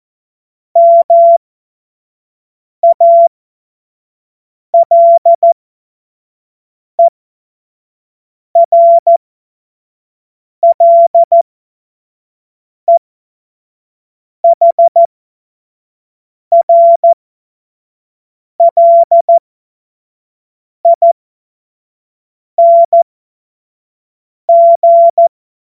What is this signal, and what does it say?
Morse code, heard as a keyed tone, about 14 wpm, spaced as for 5 wpm, 680 Hz. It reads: MALERLEHRLING
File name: Morsekurs / Witze / Witz 12 Dezember 2023 25 F70